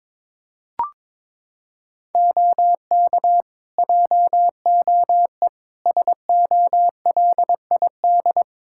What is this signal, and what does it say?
Morse code, heard as a keyed tone, about 22 wpm, 700 Hz. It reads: OK JOE SOLID
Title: Practice QSO - Exchange 3 - More Info - 22wpm